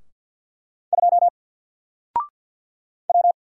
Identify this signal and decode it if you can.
Morse code, heard as a keyed tone, about 48 wpm, 700 Hz. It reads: 2 W
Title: Single Letter-Number Alt 50wpm